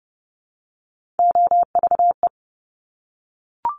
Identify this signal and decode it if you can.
Morse code, heard as a keyed tone, about 30 wpm, 700 Hz. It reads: OVE
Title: Top Three Letter Combinations in Words 30wpm